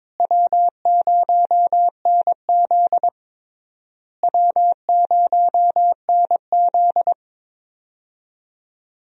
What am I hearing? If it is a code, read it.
Morse code, heard as a keyed tone, about 22 wpm, 700 Hz. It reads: W0NZ W0NZ
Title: USA Call Signs - 4 Characters - Repeated 3x - 22wpm